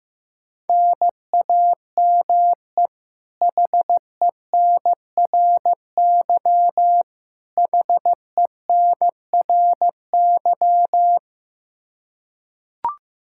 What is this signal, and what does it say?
Morse code, heard as a keyed tone, about 16 wpm, 700 Hz. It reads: NAME HENRY HENRY
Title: Practice QSO - Exchange 1 - Beginning - 15wpm